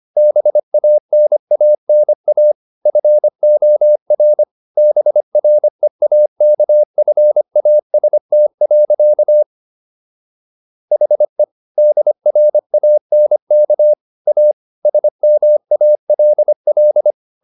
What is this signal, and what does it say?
Morse code, heard as a keyed tone, about 25 wpm, 600 Hz. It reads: BANANA FOR BREAKFAST. HE DRANK A SMALL